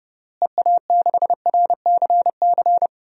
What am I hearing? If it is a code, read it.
Morse code, heard as a keyed tone, about 30 wpm, 700 Hz. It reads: EA6RCC
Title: DXCC-100 Call Signs - 6 Characters - Repeated 3x - 30wpm